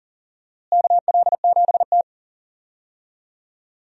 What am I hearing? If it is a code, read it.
Morse code, heard as a keyed tone, about 40 wpm, 700 Hz. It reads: KL7T
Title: USA Call Signs - 4 Characters - Repeated 2x - 40wpm